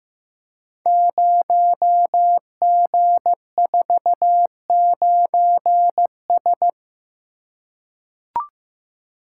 Morse code, heard as a keyed tone, about 15 wpm, 700 Hz. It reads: 0G49S
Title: Code Groups - 5 Letter:Numbers - 15wpm